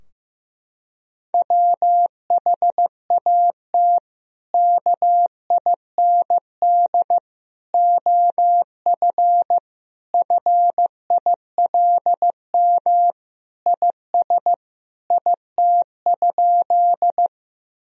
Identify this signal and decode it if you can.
Morse code, heard as a keyed tone, about 15 wpm, 700 Hz. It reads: WHAT KIND OF FILM IS IT?
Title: Sentences from Top 1000 Words - Review Part 3 - 15wpm